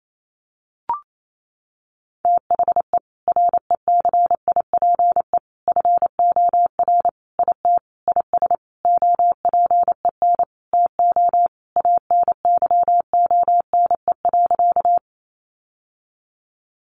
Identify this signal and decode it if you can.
Morse code, heard as a keyed tone, about 28 wpm, 700 Hz. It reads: THE RECIPE FOR IT IS OPEN TO ANYONE.
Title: Sentences from Top 1200 Words 28wpm